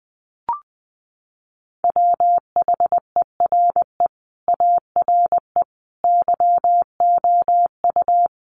Morse code, heard as a keyed tone, about 20 wpm, 700 Hz. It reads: WHERE ARE YOU
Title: Sentences from Top 200 Words 20wpm